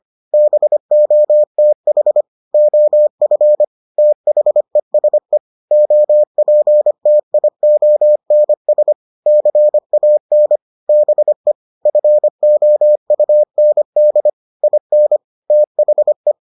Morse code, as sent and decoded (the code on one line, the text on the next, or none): -... --- - .... / --- ..-. / - .... ... / --- .--. - .. --- -. ... / -.-. .- -. / -... / ..-. --- ..- -. -.. / .. -. / - .... .
BOTH OF THESE OPTIONS CAN BE FOUND IN THE